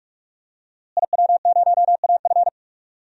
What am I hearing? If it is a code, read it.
Morse code, heard as a keyed tone, about 46 wpm, 700 Hz. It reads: IW0AF
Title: DXCC-100 Call Signs - 5 Characters - Repeated 2x - 45wpm